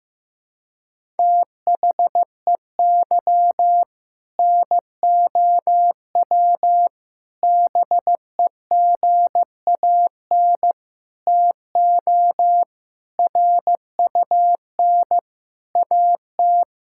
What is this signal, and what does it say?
Morse code, heard as a keyed tone, about 15 wpm, 700 Hz. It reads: THEY NOW BEGAN TO RUN AT
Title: Sentences from Top 1000 Words - Review Part 5 - 15wpm